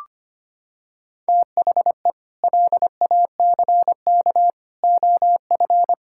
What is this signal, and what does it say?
Morse code, heard as a keyed tone, about 25 wpm, 700 Hz. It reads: THE LACK OF
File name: practice / 25wpm / Sentences with 3-Word Phrases - 25wpm